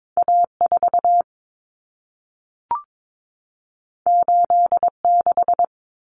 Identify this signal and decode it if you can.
Morse code, heard as a keyed tone, about 22 wpm, 700 Hz. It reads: A4 86
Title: Common Cut Numbers - 2-Digit 22wpm